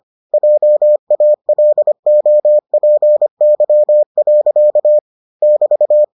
Morse code, heard as a keyed tone, about 25 wpm, 600 Hz. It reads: JALOPY. <BT>